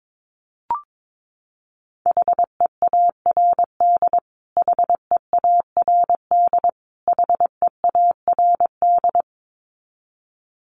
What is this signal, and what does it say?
Morse code, heard as a keyed tone, about 22 wpm, 700 Hz. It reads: HEARD HEARD HEARD
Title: Top 500 Words Repeated 22wpm